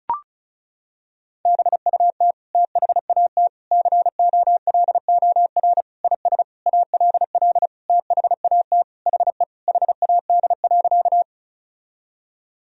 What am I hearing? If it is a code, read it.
Morse code, heard as a keyed tone, about 36 wpm, 700 Hz. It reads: BUT THAT COLOR IS ALL THAT HE HAD.
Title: Sentences from Top 1000 Words - Review Part 2 - 35wpm